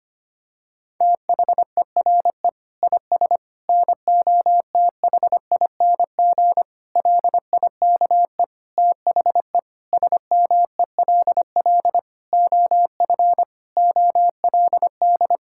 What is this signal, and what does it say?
Morse code, heard as a keyed tone, about 25 wpm, 700 Hz. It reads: THERE IS NOTHING LIKE THE SMELL OF OLD
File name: practice / 25wpm / Difficult Sentences with Hints - 25wpm